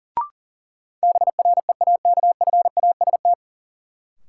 Morse code, heard as a keyed tone, about 38 wpm, 700 Hz. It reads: BREAKFAST
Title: Sentences from Top 1500 Words - New Word Drill Alt - 40wpm